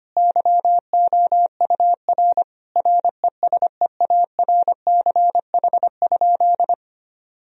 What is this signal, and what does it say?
Morse code, heard as a keyed tone, about 25 wpm, 700 Hz. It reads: YOUR RESEARCH?
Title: Sentences from Top 1000 Words - Review Part 4 - 25wpm